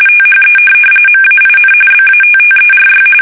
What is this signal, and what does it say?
RTTY, about 45.45 baud. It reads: THE MAIN BATTERY
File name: fsk850 / USS-Massachusetts